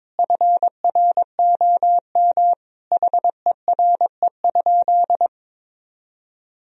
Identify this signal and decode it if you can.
Morse code, heard as a keyed tone, about 22 wpm, 700 Hz. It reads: FROM HERE?
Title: Sentences from Top 1000 Words - Review Part 4 - 22wpm